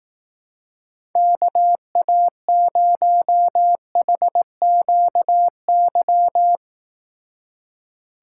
Morse code, heard as a keyed tone, about 19 wpm, 700 Hz. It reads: KA0HQY